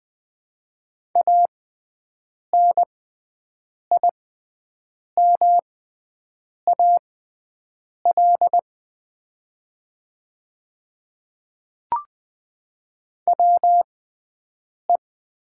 Morse code, heard as a keyed tone, about 20 wpm, spaced as for 7 wpm, 700 Hz. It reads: ANIMAL WE
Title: 6x Character Spacing - Top 200 Words - 20wpm